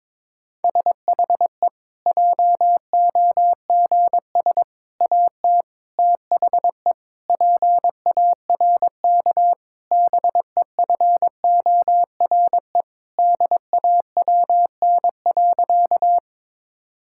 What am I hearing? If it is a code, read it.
Morse code, heard as a keyed tone, about 22 wpm, 700 Hz. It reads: SHE JOGS AT THE PARK BEFORE DAWN.
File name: Difficult Sentences with Hints - 22wpm